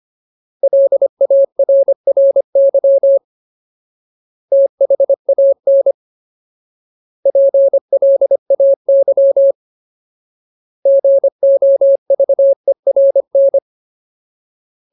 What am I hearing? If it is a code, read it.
Morse code, heard as a keyed tone, about 25 wpm, 550 Hz. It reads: LARRY THAN PLAY GOVERN
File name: Ultimate Word List / 25WPM Ultimate